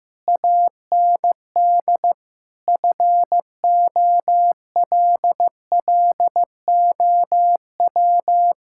Morse code, heard as a keyed tone, about 15 wpm, 700 Hz. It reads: AND FOLLOW
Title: Sentences from Top 1000 Words - Review Part 6 - 15wpm